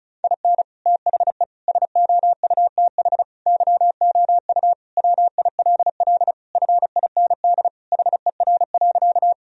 Morse code, heard as a keyed tone, about 36 wpm, 700 Hz. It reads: IN THE SOUTH YOU WILL FIND HER.
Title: Sentences from Top 1000 Words - Review Part 2 - 35wpm